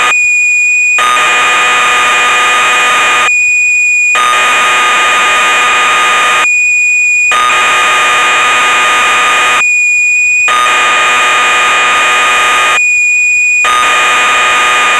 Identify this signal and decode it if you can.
Cassette data (Kansas City type).